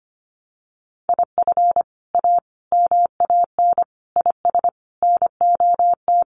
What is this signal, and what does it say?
Morse code, heard as a keyed tone, about 25 wpm, 700 Hz. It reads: IF A MAN IS NOT
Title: Sentences from Top 1400 Words 25wpm